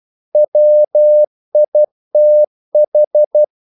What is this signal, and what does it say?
Morse code, heard as a keyed tone, about 12 wpm, 600 Hz. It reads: WITH
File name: CW / Practice / If12